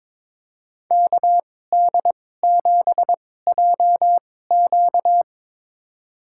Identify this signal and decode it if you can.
Morse code, heard as a keyed tone, about 22 wpm, spaced as for 16 wpm, 700 Hz. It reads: KD7JQ
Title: USA Call Signs - 5 Characters - 2x Character Spacing - 22wpm